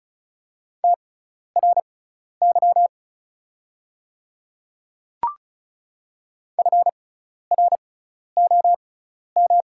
Morse code, heard as a keyed tone, about 34 wpm, spaced as for 12 wpm, 700 Hz. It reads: TRY FROM
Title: 6x Character Spacing - Top 500 Words - 35wpm